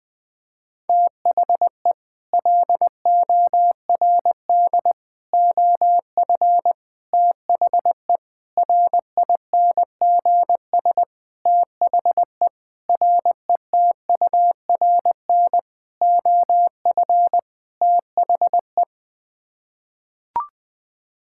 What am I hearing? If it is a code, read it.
Morse code, heard as a keyed tone, about 20 wpm, 700 Hz. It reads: THE LORD OF THE RINGS THE RETURN OF THE
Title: Sets of 9 Words - 20wpm